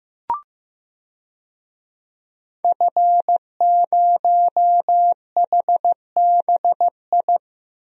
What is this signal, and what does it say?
Morse code, heard as a keyed tone, about 15 wpm, 700 Hz. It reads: F0HBI
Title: DXCC-100 Call Signs - 5 Characters - 15wpm